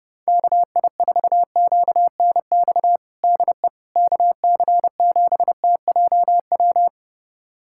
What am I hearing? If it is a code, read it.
Morse code, heard as a keyed tone, about 32 wpm, 700 Hz. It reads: KI4QNX DE KC7TJW
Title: USA Callsign DE Callsign - 6 Characters - 30wpm